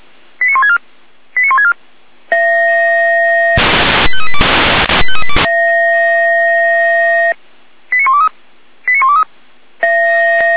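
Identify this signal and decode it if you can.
Five-tone selective call (ZVEI), heard as quick runs of five tones, tones of about 70 ms. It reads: 89164 89164 85382 85382 89123 89123
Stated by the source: ZVEI five-tone call 89100 received on an SDR.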